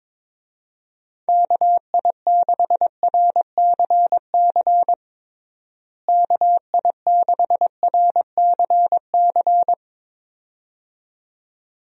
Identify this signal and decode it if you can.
Morse code, heard as a keyed tone, about 21 wpm, 700 Hz. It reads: KI6RCC KI6RCC
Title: USA Call Signs - 6 Characters - Repeated 2x - 22wpm